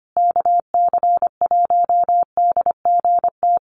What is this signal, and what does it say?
Morse code, heard as a keyed tone, about 25 wpm, 700 Hz. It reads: KC1DGT
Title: USA Call Signs Repeated 3x 25wpm